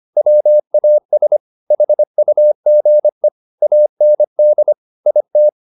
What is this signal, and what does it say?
Morse code, heard as a keyed tone, about 25 wpm, 600 Hz. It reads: WAS HUGE AND IT